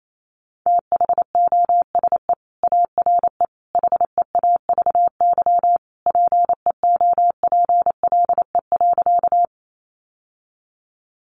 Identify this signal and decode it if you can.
Morse code, heard as a keyed tone, about 28 wpm, 700 Hz. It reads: THOSE ARE HEAVY PEOPLE.
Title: Sentences from Top 1100 Words 28wpm